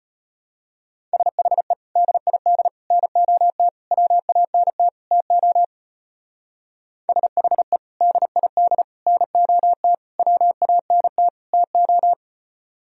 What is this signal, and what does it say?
Morse code, heard as a keyed tone, about 29 wpm, 700 Hz. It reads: SHE DID NOT WANT TO SHE DID NOT WANT TO
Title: Speed-Racing - Sets of 5 Words 25wpm